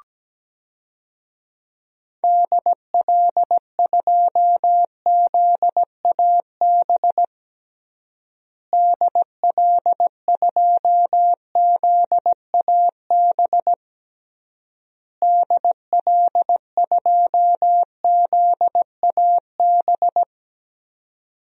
Morse code, heard as a keyed tone, about 17 wpm, 700 Hz. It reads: DL2ZAB DL2ZAB DL2ZAB
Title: DXCC-100 Call Signs - 6 Characters - Repeated 3x - 17wpm